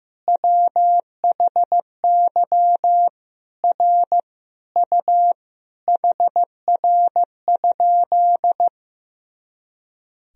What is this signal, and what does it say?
Morse code, heard as a keyed tone, about 15 wpm, 700 Hz. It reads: WHY R U HR?
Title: Sentences from Top 200 Words and Abbreviations 15wpm